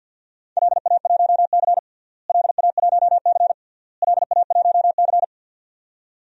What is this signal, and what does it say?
Morse code, heard as a keyed tone, about 50 wpm, 700 Hz. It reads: LA1C LA1C LA1C
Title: DXCC-100 Call Signs - 4 Characters - Repeated 3x - 50wpm